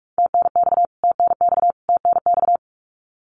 Morse code, heard as a keyed tone, about 44 wpm, 700 Hz. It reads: TNX TNX TNX
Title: Abbreviations Repeated 3x - 45wpm